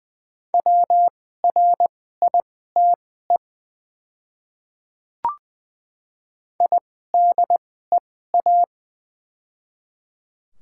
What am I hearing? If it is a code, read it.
Morse code, heard as a keyed tone, about 20 wpm, spaced as for 14 wpm, 700 Hz. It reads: WRITE IDEA
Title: 2x Character Spacing - Top 500 Words - 20wpm